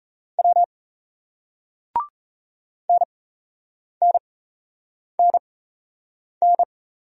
Morse code, heard as a keyed tone, about 38 wpm, spaced as for 8 wpm, 700 Hz. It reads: W NNNN